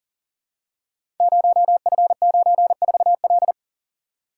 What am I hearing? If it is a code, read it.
Morse code, heard as a keyed tone, about 38 wpm, 700 Hz. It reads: 0F94L
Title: Code Groups - 5 Letter:Numbers - 40wpm